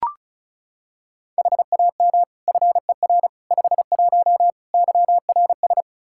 Morse code, heard as a keyed tone, about 36 wpm, 700 Hz. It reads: HAM FER 51 YRS
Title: Practice QSO - Exchange 3 - More Info - 35wpm